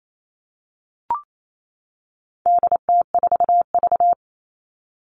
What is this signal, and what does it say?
Morse code, heard as a keyed tone, about 28 wpm, 700 Hz. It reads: DT4V
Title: All Cut Numbers - 4-Digit 28wpm